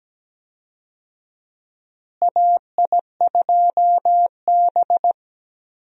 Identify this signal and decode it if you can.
Morse code, heard as a keyed tone, about 17 wpm, 700 Hz. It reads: AI2B
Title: USA Call Signs - 4 Characters - Repeated 2x - 17wpm